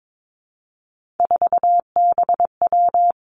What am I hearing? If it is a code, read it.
Morse code, heard as a keyed tone, about 22 wpm, 700 Hz. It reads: 4BW
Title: Three Challenging Letter-Number Combinations 22wpm